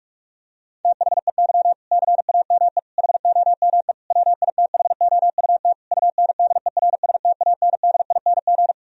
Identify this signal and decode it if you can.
Morse code, heard as a keyed tone, about 46 wpm, 700 Hz. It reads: THEY CAME HOME WITHOUT UNDERSTANDING
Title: Sentences from Top 1000 Words - Review Part 3 - 45wpm